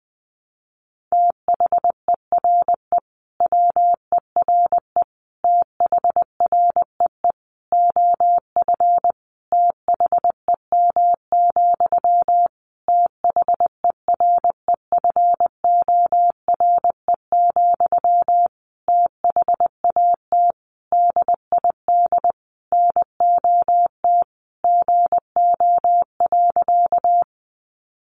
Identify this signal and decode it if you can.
Morse code, heard as a keyed tone, about 20 wpm, 700 Hz. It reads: THERE WERE THREE OF THEM, THEREFORE, THAT DID NOT GO.
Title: Sentences from Top 800 Words 20wpm